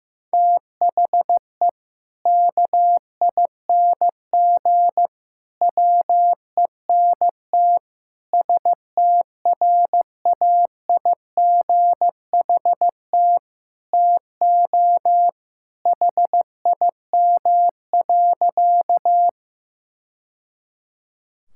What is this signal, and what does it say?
Morse code, heard as a keyed tone, about 16 wpm, 700 Hz. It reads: THE KING WENT STRAIGHT TO HIM.
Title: Sentences from Top 1000 Words - Review Part 6 - 15wpm